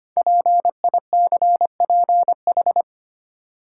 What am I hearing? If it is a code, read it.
Morse code, heard as a keyed tone, about 25 wpm, 700 Hz. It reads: PICPH
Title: Code Groups - 5 Letters - 25wpm